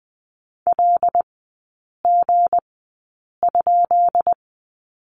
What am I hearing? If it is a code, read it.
Morse code, heard as a keyed tone, about 19 wpm, spaced as for 8 wpm, 700 Hz. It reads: LG?
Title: 159 Lesson 26e - Letter g - Three Characters 20wpm